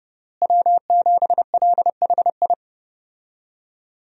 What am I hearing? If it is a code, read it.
Morse code, heard as a keyed tone, about 30 wpm, 700 Hz. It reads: W7LHI
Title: USA Call Signs - 5 Characters - Repeated 3x - 30wpm